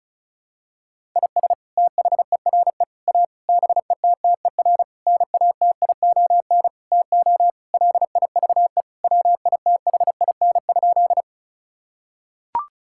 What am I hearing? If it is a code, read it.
Morse code, heard as a keyed tone, about 36 wpm, 700 Hz. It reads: IS THERE A BETTER NATION TO LIVE WITHIN?
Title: Sentences from Top 1000 Words - Review Part 6 - 35wpm